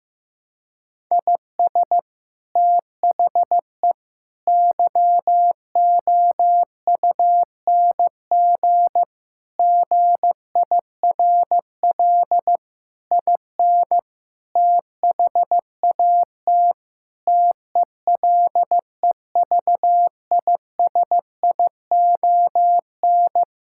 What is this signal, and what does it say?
Morse code, heard as a keyed tone, about 15 wpm, 700 Hz. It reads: IS THE YOUNG GIRL IN THAT TELEVISION
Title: Sentences from Top 1000 Words - Review Part 5 - 15wpm